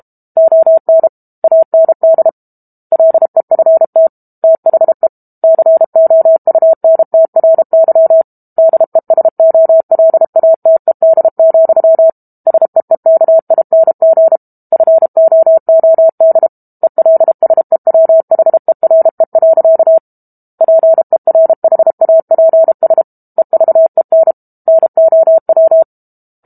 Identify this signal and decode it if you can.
Morse code, heard as a keyed tone, about 32 wpm, 650 Hz. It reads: ON AND LEFT THE COUNTRY DESOLATED, SEEKING FOOD ELSEWHERE. PERHAPS EVEN NOW
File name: Book Two / 06-Chapter 6 - 32 WPM